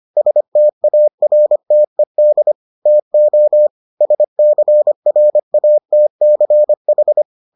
Morse code, heard as a keyed tone, about 25 wpm, 600 Hz. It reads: STARTED TO SCRATCH